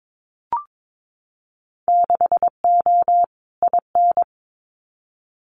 Morse code, heard as a keyed tone, about 22 wpm, 700 Hz. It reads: 6O IN
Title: Winter Field Day Exchanges Encore 22wpm